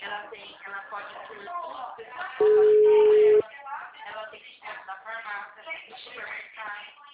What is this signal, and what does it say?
Ringback tone (Europe, 425 Hz), with speech.